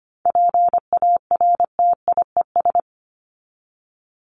Morse code, heard as a keyed tone, about 25 wpm, 700 Hz. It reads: PARTIES
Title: ES:S Suffix - Top 100 Words with Base Words 25wpm